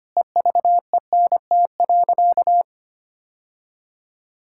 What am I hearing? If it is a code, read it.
Morse code, heard as a keyed tone, about 25 wpm, 700 Hz. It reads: EVENT.
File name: practice / 25wpm / Difficult Sentences with Hints - 25wpm